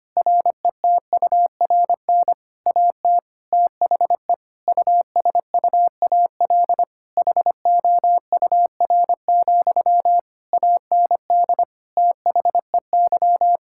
Morse code, heard as a keyed tone, about 25 wpm, 700 Hz. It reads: RETURN AT THE USUAL HOUR, AND THEY
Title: Sentences from Top 1600 Words 25wpm